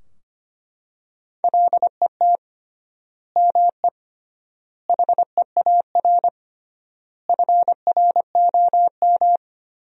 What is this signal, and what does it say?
Morse code, heard as a keyed tone, about 25 wpm, 700 Hz. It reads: LET ME HEAR FROM
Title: 3x Word-Spacing - Sentences from Top 400 Words - 25wpm